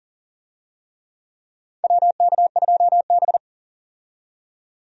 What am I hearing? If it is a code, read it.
Morse code, heard as a keyed tone, about 40 wpm, 700 Hz. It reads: WK2B
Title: USA Call Signs - 4 Characters - 40wpm